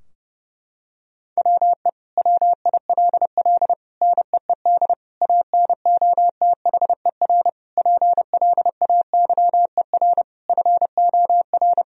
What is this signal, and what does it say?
Morse code, heard as a keyed tone, about 30 wpm, 700 Hz. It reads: WE WILL NEED ANOTHER PLAYER FOR
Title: Sentences from Top 1000 Words - Review Part 4 - 30wpm